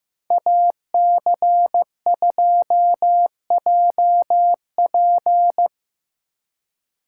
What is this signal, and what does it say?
Morse code, heard as a keyed tone, about 17 wpm, 700 Hz. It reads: AC2JP
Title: Speed Racing - USA 2x2 Call Signs 15wpm